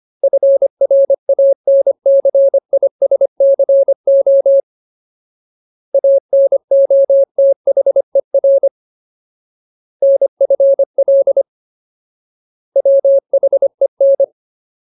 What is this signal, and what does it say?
Morse code, heard as a keyed tone, about 25 wpm, 550 Hz. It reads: FRANCISCO ANOTHER NFL WHEN
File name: Ultimate Word List / 25WPM Ultimate